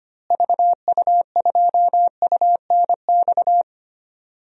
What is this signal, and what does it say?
Morse code, heard as a keyed tone, about 25 wpm, 700 Hz. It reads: VU2UNX
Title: DXCC-100 Call Signs - 6 Characters - Repeated 3x - 25wpm